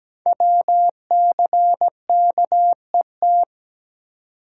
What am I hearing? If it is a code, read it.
Morse code, heard as a keyed tone, about 17 wpm, 700 Hz. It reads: WCKET
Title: Code Groups - 5 Letters - 17wpm